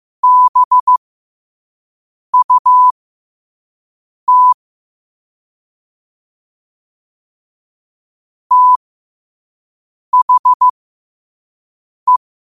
Morse code, heard as a keyed tone, about 15 wpm, spaced as for 5 wpm, 1000 Hz. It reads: BUT THE
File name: qotdfiles / 5wpm